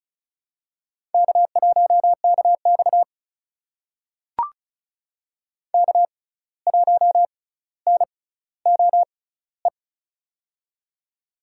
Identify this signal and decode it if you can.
Morse code, heard as a keyed tone, about 36 wpm, spaced as for 17 wpm, 700 Hz. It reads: K1KX K1NOE
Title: USA Call Signs - 6x Character Spacing - 35wpm